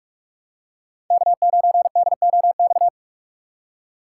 Morse code, heard as a keyed tone, about 46 wpm, 700 Hz. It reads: K9DOX
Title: USA Call Signs - 5 Characters - 2x Character Spacing - 45wpm